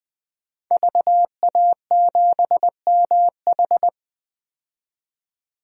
Morse code, heard as a keyed tone, about 20 wpm, 700 Hz. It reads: VA7MH
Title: DXCC-100 Call Signs - 5 Characters - 20wpm